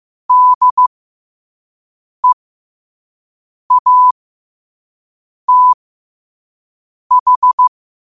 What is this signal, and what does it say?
Morse code, heard as a keyed tone, about 15 wpm, spaced as for 5 wpm, 1000 Hz. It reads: DEATH
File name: qotdfiles / 5wpm